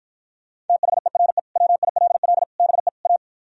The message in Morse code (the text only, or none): THERE WILL BE A